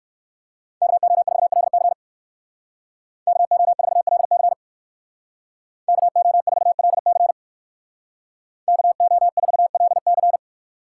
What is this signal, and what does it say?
Morse code, heard as a keyed tone, about 59 wpm, 700 Hz. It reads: KO4LC KO4LC KO4LC KO4LC